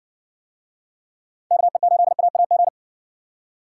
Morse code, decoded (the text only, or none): KE8AAG